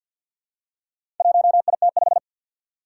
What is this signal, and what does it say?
Morse code, heard as a keyed tone, about 52 wpm, 700 Hz. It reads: 1IT5
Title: Four Challenging Letter-Number Combinations 50wpm